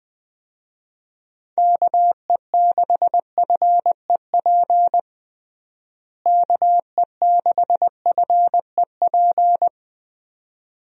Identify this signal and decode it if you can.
Morse code, heard as a keyed tone, about 21 wpm, 700 Hz. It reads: KE6FEP KE6FEP
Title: USA Call Signs - 6 Characters - Repeated 2x - 20wpm